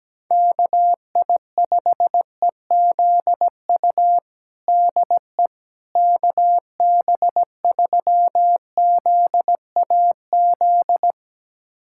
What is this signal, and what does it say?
Morse code, heard as a keyed tone, about 17 wpm, 700 Hz. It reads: KI5EZU DE KB3ZAZ
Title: USA Callsign DE Callsign - 6 Characters - 17wpm